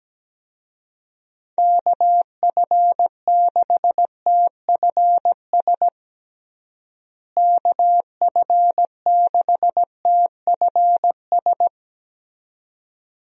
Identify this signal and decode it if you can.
Morse code, heard as a keyed tone, about 17 wpm, 700 Hz. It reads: KF6TFS KF6TFS
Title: USA Call Signs - 6 Characters - Repeated 2x - 17wpm